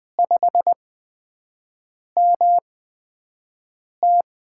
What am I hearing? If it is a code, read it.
Morse code, heard as a keyed tone, about 20 wpm, spaced as for 5 wpm, 700 Hz. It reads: 5MT
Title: USA Call Signs - 5 Characters - 8x Character Spacing - 20wpm